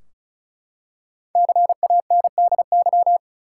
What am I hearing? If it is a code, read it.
Morse code, heard as a keyed tone, about 36 wpm, 700 Hz. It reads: CANDY
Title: Five Letter Words - 35wpm